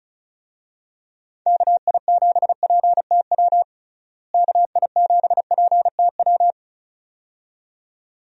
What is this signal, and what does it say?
Morse code, heard as a keyed tone, about 34 wpm, 700 Hz. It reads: KI7PTW KI7PTW
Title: USA Call Signs - 6 Characters - Repeated 2x - 35wpm